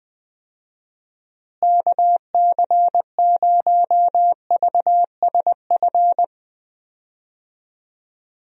Morse code, heard as a keyed tone, about 20 wpm, 700 Hz. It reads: KC0VSF
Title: USA Call Signs - 6 Characters - Repeated 2x - 20wpm